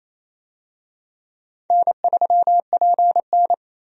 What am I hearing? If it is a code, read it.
Morse code, heard as a keyed tone, about 28 wpm, 700 Hz. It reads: N3PN
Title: USA Call Signs - 4 Characters - 28wpm